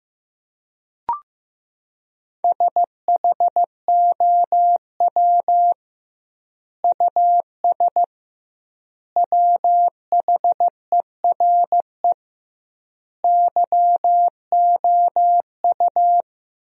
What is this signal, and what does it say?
Morse code, heard as a keyed tone, about 15 wpm, 700 Hz. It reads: SHOW US WHERE YOU
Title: 2x Word-Spacing - Sentences from Top 200 Words - 15wpm